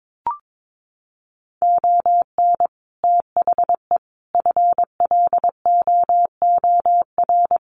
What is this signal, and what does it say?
Morse code, heard as a keyed tone, about 22 wpm, 700 Hz. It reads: ON THE FLOOR
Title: Sets of 5 Words 22wpm